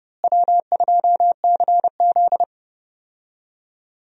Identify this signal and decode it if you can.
Morse code, heard as a keyed tone, about 30 wpm, 700 Hz. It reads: W2CZ
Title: USA Call Signs - 4 Characters - Repeated 2x - 30wpm